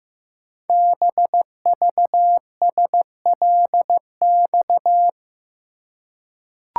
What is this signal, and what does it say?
Morse code, heard as a keyed tone, about 15 wpm, 700 Hz. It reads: BVSLX
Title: Code Groups - 5 Letter:Numbers - 15wpm